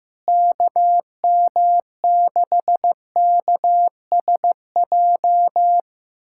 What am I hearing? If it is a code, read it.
Morse code, heard as a keyed tone, about 15 wpm, 700 Hz. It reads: KM6KSJ
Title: USA Callsign DE Callsign - 6 Characters - 15wpm